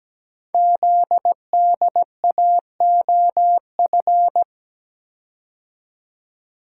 Morse code, heard as a keyed tone, about 17 wpm, 700 Hz. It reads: ZDAOF
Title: Code Groups - 5 Letter:Numbers - 17wpm